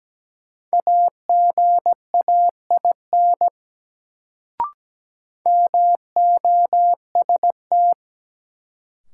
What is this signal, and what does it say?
Morse code, heard as a keyed tone, about 17 wpm, 700 Hz. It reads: AGAIN MOST
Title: Sentences from Top 200 Words - New Word Drill - 17wpm